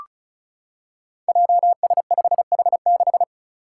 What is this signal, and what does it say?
Morse code, heard as a keyed tone, about 36 wpm, 700 Hz. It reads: JS5H6
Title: Five Challenging Letter-Number Combinations 35wpm